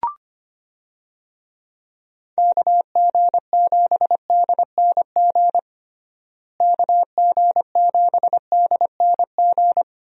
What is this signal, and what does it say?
Morse code, heard as a keyed tone, about 25 wpm, 700 Hz. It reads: KG7DNG KG7DNG
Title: USA Call Signs - 6 Characters - Repeated 2x - 25wpm